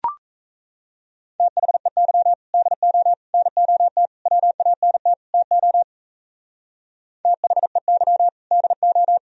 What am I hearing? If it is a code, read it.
Morse code, heard as a keyed tone, about 33 wpm, 700 Hz. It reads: THEY DO NOT WANT TO THEY DO
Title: Speed-Racing - Sets of 5 Words 28wpm